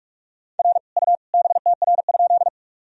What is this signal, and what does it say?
Morse code, heard as a keyed tone, about 48 wpm, 700 Hz. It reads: R U BTR?
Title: Sentences from Top 400 Words and Abbreviations 45wpm